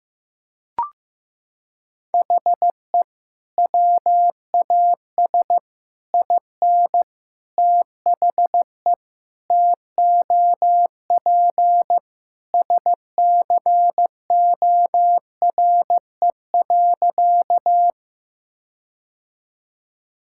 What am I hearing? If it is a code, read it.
Morse code, heard as a keyed tone, about 15 wpm, 700 Hz. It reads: HE WAS IN THE TOP SCORE.
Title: Sentences from Top 1600 Words 15wpm